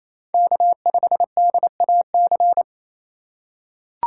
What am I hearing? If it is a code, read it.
Morse code, heard as a keyed tone, about 28 wpm, 700 Hz. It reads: K5DAC
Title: USA Call Signs - 6x Character Spacing - 28wpm